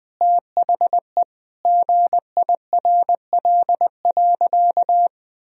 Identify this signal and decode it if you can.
Morse code, heard as a keyed tone, about 20 wpm, 700 Hz. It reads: THE GIRL.
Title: Sentences from Top 400 Words and Abbreviations 20wpm